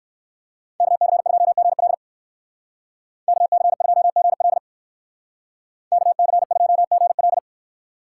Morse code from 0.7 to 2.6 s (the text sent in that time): KC2GL